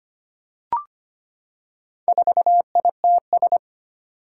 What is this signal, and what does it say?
Morse code, heard as a keyed tone, about 25 wpm, 700 Hz. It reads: 4ITS
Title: Four Challenging Letter-Number Combinations 25wpm